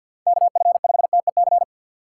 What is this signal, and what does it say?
Morse code, heard as a keyed tone, about 47 wpm, 700 Hz. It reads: KF5TEC
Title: USA Call Signs - 6 Characters - 3x Character Spacing - 50wpm